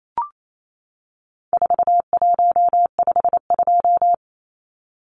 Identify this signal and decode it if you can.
Morse code, heard as a keyed tone, about 28 wpm, 700 Hz. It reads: 4152
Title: Numbers - 4-Digit 28wpm